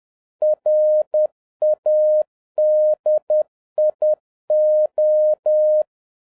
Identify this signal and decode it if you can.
Morse code, heard as a keyed tone, about 10 wpm, 610 Hz. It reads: RADIO